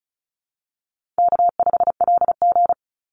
Morse code, heard as a keyed tone, about 34 wpm, 700 Hz. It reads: K5LG
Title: Call Signs - 35wpm